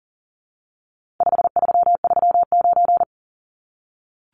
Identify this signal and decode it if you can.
Morse code, heard as a keyed tone, about 40 wpm, 700 Hz. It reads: 5339
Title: Numbers - 4-Digit 40wpm